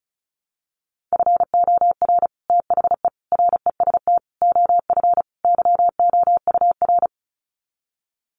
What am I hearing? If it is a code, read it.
Morse code, heard as a keyed tone, about 34 wpm, 700 Hz. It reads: FOR THE REST OF YOUR
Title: Sets of 5 Words Encore 35wpm